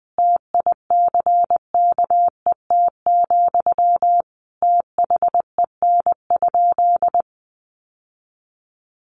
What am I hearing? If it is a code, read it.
Morse code, heard as a keyed tone, about 20 wpm, 700 Hz. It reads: TICKET, THEN?
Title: Sentences from Top 1400 Words 20wpm